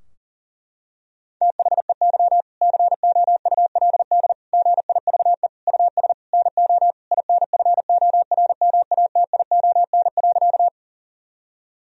Morse code, heard as a keyed tone, about 37 wpm, 700 Hz. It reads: THEY COULD GIVE US NO INFORMATION.
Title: Sentences from Top 1000 Words - Review Part 3 - 40wpm